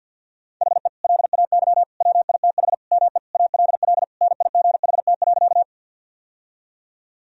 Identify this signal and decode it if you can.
Morse code, heard as a keyed tone, about 50 wpm, 700 Hz. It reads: HE LAY WITH ME ALL NIGHT.
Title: Sentences from Top 1000 Words - Review Part 2 - 50wpm